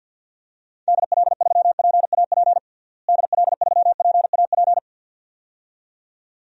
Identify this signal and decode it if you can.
Morse code, heard as a keyed tone, about 53 wpm, 700 Hz. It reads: DL3PAP DL3PAP
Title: DXCC-100 Call Signs - 6 Characters - Repeated 2x - 50wpm